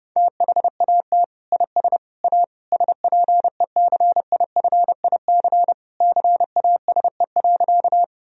Morse code, heard as a keyed tone, about 32 wpm, 700 Hz. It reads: THAT IS A SPECIFIC CASE.
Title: Sentences from Top 1000 Words - Review Part 3 - 30wpm